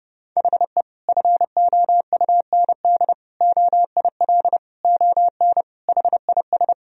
Morse code, heard as a keyed tone, about 32 wpm, 700 Hz. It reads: HE FOUND OIL ON HIS
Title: Sentences from Top 1000 Words - Review Part 4 - 30wpm